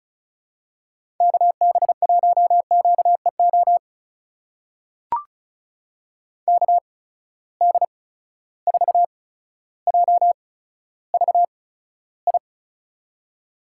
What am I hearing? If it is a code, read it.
Morse code, heard as a keyed tone, about 36 wpm, spaced as for 15 wpm, 700 Hz. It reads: KB1QEO KD4JVI
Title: USA Call Signs - 6 Characters - 8x Character Spacing - 35wpm